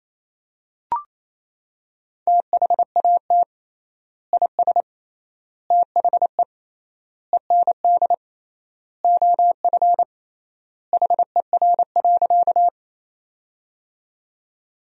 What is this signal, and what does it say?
Morse code, heard as a keyed tone, about 28 wpm, 700 Hz. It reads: THAT IS THE END OF HER.
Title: 3x Word-Spacing - Sentences from Top 200 Words - 28wpm